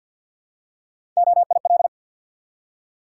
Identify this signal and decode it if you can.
Morse code, heard as a keyed tone, about 49 wpm, 700 Hz. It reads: OIL
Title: Three Letter Words - 50wpm